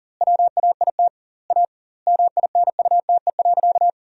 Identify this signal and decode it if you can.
Morse code, heard as a keyed tone, about 40 wpm, 700 Hz. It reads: WAIT A MINUTE.
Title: 2x Word-Spacing - Sentences from Top 500 Words - 40wpm